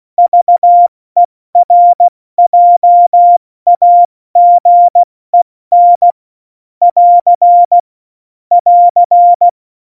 Morse code, heard as a keyed tone, about 16 wpm, 700 Hz. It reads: VERJAGEN <AR> <AR>
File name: Morsekurs / Witze / Witz 11 Juli 17, Tempo 70 F80